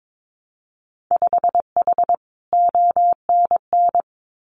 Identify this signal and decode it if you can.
Morse code, heard as a keyed tone, about 22 wpm, 700 Hz. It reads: 5H ONN
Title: Winter Field Day Exchanges Encore 22wpm